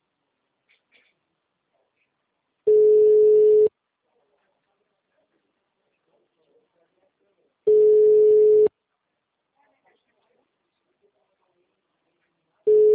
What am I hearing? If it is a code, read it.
Ringback tone (Europe, 425 Hz).